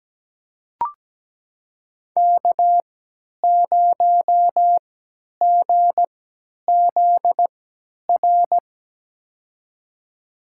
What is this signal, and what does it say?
Morse code, heard as a keyed tone, about 17 wpm, spaced as for 10 wpm, 700 Hz. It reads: K0GZR
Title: USA Call Signs - 3x Character Spacing - 17wpm